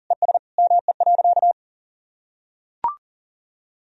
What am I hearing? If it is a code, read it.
Morse code, heard as a keyed tone, about 40 wpm, 700 Hz. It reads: ES ME.
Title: Sentences from Top 500 Words and Abbreviations 40wpm